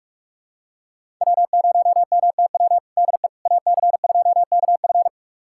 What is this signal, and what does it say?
Morse code, heard as a keyed tone, about 46 wpm, 700 Hz. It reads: W0MTW DE AC2KF
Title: USA Callsign DE Callsign - 5 Characters - 45wpm